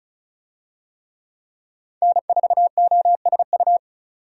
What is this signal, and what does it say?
Morse code, heard as a keyed tone, about 36 wpm, 700 Hz. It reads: N4OSU
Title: USA Call Signs - 5 Characters - 35wpm